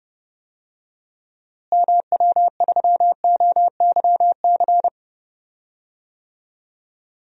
Morse code, heard as a keyed tone, about 31 wpm, 700 Hz. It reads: MW3OYC
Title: DXCC-100 Call Signs - 6 Characters - Repeated 2x - 30wpm